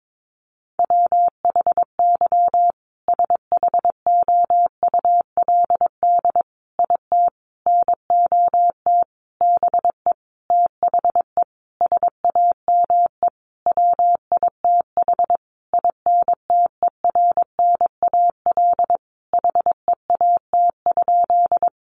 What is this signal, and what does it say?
Morse code, heard as a keyed tone, about 22 wpm, 700 Hz. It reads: WHY SHOULD IT NOT BE THE SAME WITH INTERNAL HEAT?
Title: Sentences from Top 1400 Words 22wpm